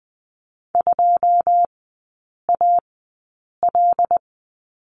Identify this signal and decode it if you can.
Morse code, heard as a keyed tone, about 20 wpm, spaced as for 8 wpm, 690 Hz. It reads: 2AL
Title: 081 Lesson 14e - Number 2 - Three Characters 20wpm